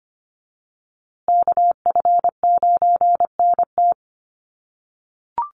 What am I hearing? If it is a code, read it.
Morse code, heard as a keyed tone, about 25 wpm, 700 Hz. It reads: KF9NT
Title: USA Call Signs Repeated 2x 25wpm